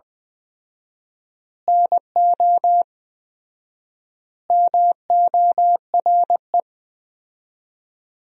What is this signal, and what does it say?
Morse code, heard as a keyed tone, about 20 wpm, 700 Hz. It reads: NO MORE
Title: 4x Word-Spacing - Sentences from Top 200 Words - 20wpm